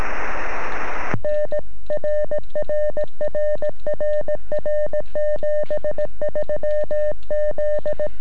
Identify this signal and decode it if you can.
Morse code, heard as a keyed tone, about 17 wpm, 600 Hz. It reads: N RRRRR 73 Z
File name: sm7lcb / 2121 sm7gep 9cm